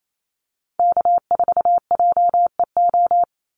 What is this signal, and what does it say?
Morse code, heard as a keyed tone, about 28 wpm, 700 Hz. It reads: K4JEO
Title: USA Call Signs 28wpm